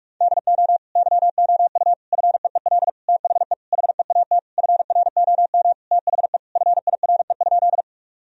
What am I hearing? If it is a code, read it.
Morse code, heard as a keyed tone, about 41 wpm, 700 Hz. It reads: DO YOU FEEL THE HEAT FROM THE FIRE?
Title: Sentences from Top 1000 Words - Review Part 3 - 45wpm